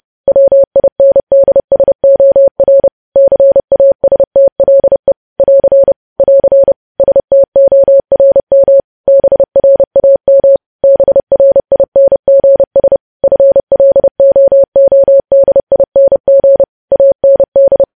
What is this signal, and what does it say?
Morse code, heard as a keyed tone, about 30 wpm, 550 Hz. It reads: WINDSOR CASTLE <AR> <AR> STORM BRAM BRINGS FLOODING AND